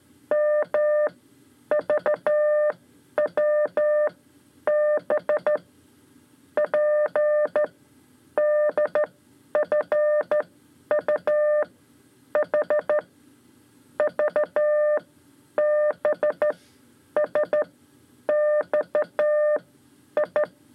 Morse code, heard as a keyed tone, about 13 wpm, spaced as for 10 wpm, 600 Hz. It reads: M VW B P DFU H V B S X I
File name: 5 Lezione CW 2